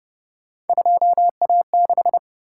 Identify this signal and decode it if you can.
Morse code, heard as a keyed tone, about 31 wpm, 700 Hz. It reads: 2A6